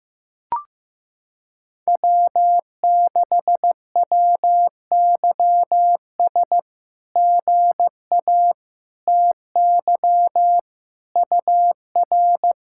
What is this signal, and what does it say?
Morse code, heard as a keyed tone, about 15 wpm, 700 Hz. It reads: W6WYS GA TY UR
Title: SOTA QSO Practice 15wpm